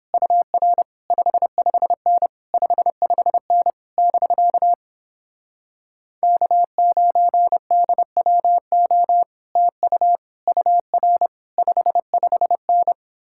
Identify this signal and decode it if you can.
Morse code, heard as a keyed tone, about 29 wpm, 700 Hz. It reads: UR 55N 55N <BK> K9DWO TU UR 55N